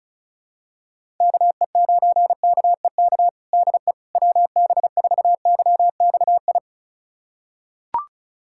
Morse code, heard as a keyed tone, about 36 wpm, 700 Hz. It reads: KE9KEK DE WB4YXI
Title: USA Callsign DE Callsign - 6 Characters - 35wpm